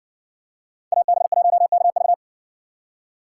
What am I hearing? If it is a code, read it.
Morse code, heard as a keyed tone, about 60 wpm, 700 Hz. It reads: AB1KV